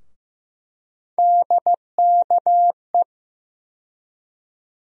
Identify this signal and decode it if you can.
Morse code, heard as a keyed tone, about 15 wpm, 700 Hz. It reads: DKE
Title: USA Call Sign Postfixes - 3 Letters 15wpm